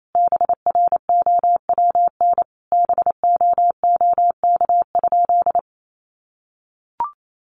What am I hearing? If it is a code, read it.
Morse code, heard as a keyed tone, about 28 wpm, 700 Hz. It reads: BROWN BOOK?
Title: Sentences from Top 1200 Words 28wpm